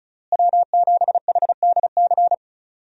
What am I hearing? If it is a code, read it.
Morse code, heard as a keyed tone, about 34 wpm, 700 Hz. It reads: W7HDC
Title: USA Call Signs - 5 Characters - Repeated 2x - 35wpm